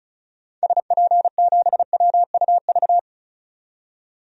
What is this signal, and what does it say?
Morse code, heard as a keyed tone, about 36 wpm, 700 Hz. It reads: SP7WUV
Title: DXCC-100 Call Signs - 6 Characters - 35wpm